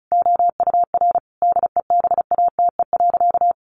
Morse code, heard as a keyed tone, about 34 wpm, 700 Hz. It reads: OUR DEBATE.
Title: Sentences from Top 1500 Words 35wpm